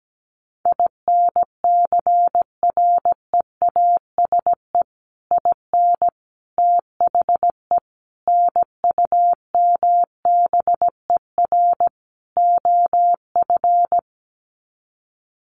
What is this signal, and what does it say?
Morse code, heard as a keyed tone, about 17 wpm, 700 Hz. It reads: INCREASE IN THE NUMBER OF
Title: Sets of 5 Words 17wpm